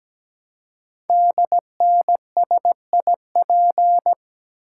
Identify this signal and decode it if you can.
Morse code, heard as a keyed tone, about 17 wpm, 700 Hz. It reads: DNSIP
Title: Code Groups - 5 Letters - 17wpm